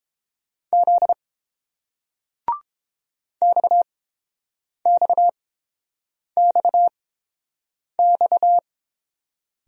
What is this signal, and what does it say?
Morse code, heard as a keyed tone, about 29 wpm, spaced as for 7 wpm, 690 Hz. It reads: Z XXXX